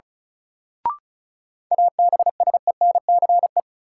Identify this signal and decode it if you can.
Morse code, heard as a keyed tone, about 36 wpm, 700 Hz. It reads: ABSENCE
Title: Sets of 4 Words - New Word Drill - 35wpm